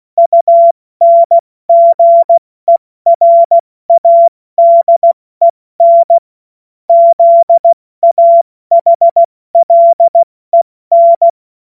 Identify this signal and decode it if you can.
Morse code, heard as a keyed tone, about 16 wpm, 680 Hz. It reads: UNGERADEN ZAHLEN
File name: Morsekurs / Witze / Witz 12. April 2022 70 F80